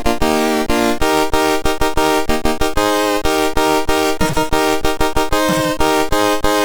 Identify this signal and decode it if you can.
There is no code.